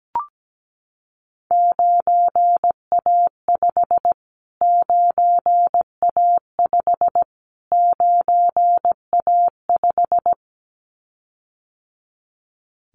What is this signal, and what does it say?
Morse code, heard as a keyed tone, about 17 wpm, 700 Hz. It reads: 9A5 9A5 9A5
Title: Callsign Prefixes for Top 25 DXCC - Repeated 3x - 17wpm